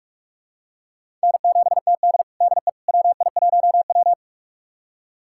Morse code, heard as a keyed tone, about 46 wpm, 700 Hz. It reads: N7TD DE WI1W
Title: USA Callsign DE Callsign - 4 Characters - 45wpm